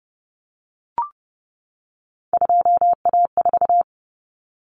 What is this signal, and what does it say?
Morse code, heard as a keyed tone, about 30 wpm, 700 Hz. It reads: 2A4
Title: Common Cut Numbers - 3-Digit 30wpm